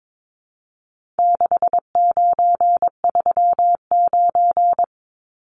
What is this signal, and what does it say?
Morse code, heard as a keyed tone, about 22 wpm, 700 Hz. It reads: 6939
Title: Numbers - 4-Digit 22wpm